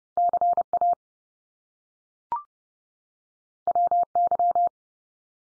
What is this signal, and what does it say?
Morse code, heard as a keyed tone, about 30 wpm, 700 Hz. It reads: CA WY
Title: US States 30WPM